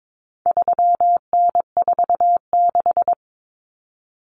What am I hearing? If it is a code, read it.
Morse code, heard as a keyed tone, about 22 wpm, 700 Hz. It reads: 3N46
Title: Common Cut Numbers - 4-Digit 22wpm